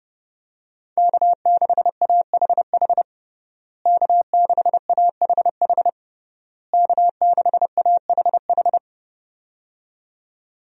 Morse code, heard as a keyed tone, about 30 wpm, 700 Hz. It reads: K6AHH K6AHH K6AHH
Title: USA Call Signs - 5 Characters - Repeated 3x - 30wpm